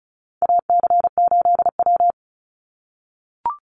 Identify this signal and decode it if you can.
Morse code, heard as a keyed tone, about 36 wpm, 700 Hz. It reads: AC8W
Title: Call Signs - 35wpm